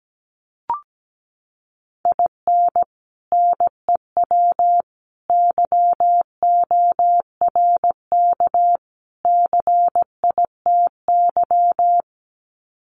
Morse code, heard as a keyed tone, about 17 wpm, 700 Hz. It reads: IN NEW YORK CITY
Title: Sets of 4 Words 17wpm